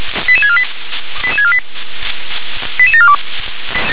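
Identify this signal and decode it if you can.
Five-tone selective call (ZVEI), heard as quick runs of five tones, tones of about 70 ms. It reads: 88648 88648 88742
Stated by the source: ZVEI five-tone selective call received on an SDR.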